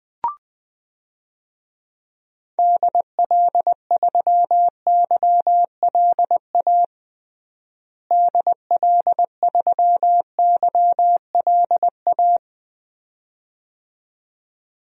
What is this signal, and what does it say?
Morse code, heard as a keyed tone, about 20 wpm, 700 Hz. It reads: DL3YLA DL3YLA
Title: DXCC-100 Call Signs - 6 Characters - Repeated 2x - 20wpm